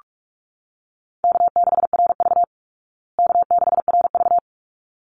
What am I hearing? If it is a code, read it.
Morse code, heard as a keyed tone, about 44 wpm, 700 Hz. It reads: K6RV K6RV
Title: Easy Call Signs Repeated 2x - 45wpm